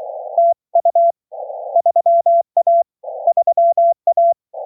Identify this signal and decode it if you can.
Morse code, heard as a keyed tone, about 23 wpm, 690 Hz. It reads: TU 3A 3A